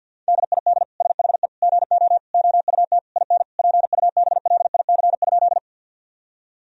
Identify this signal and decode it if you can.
Morse code, heard as a keyed tone, about 49 wpm, 700 Hz. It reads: DID SHE GO OUT IN PUBLIC?
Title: Sentences from Top 1000 Words - Review Part 3 - 50wpm